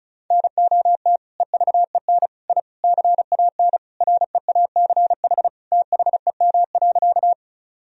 Morse code, heard as a keyed tone, about 34 wpm, 700 Hz. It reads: NOT EVEN I CAN REACH THEM.
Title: Sentences from Top 1000 Words - Review Part 2 - 35wpm